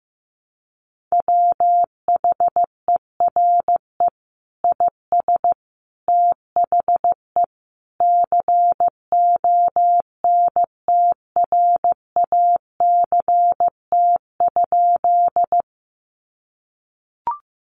Morse code, heard as a keyed tone, about 15 wpm, 700 Hz. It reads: WHERE IS THE CONTRACT?
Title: Sentences from Top 1400 Words 15wpm